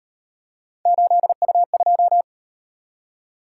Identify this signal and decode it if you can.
Morse code, heard as a keyed tone, about 37 wpm, 700 Hz. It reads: 8U2